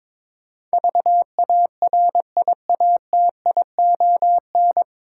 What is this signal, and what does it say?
Morse code, heard as a keyed tone, about 22 wpm, 700 Hz. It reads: VARIATION
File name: 9 Letter Words - First 3 Letters Given - 22wpm